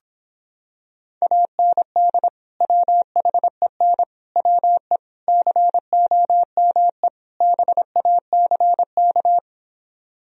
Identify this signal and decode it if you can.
Morse code, heard as a keyed tone, about 26 wpm, 700 Hz. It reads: AND WHEN WE COME BACK